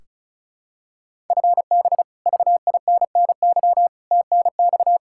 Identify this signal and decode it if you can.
Morse code, heard as a keyed tone, about 34 wpm, 700 Hz. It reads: FB VINNY TNX
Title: Practice QSO - Exchange 3 - More Info - 35wpm